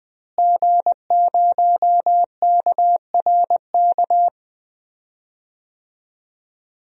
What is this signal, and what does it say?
Morse code, heard as a keyed tone, about 20 wpm, 700 Hz. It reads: G0KRK
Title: DXCC-100 Call Signs - 5 Characters - Repeated 3x - 20wpm